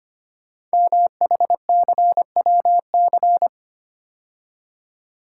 Morse code, heard as a keyed tone, about 25 wpm, 700 Hz. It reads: MHCWC
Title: Code Groups - 5 Letter:Numbers - 25wpm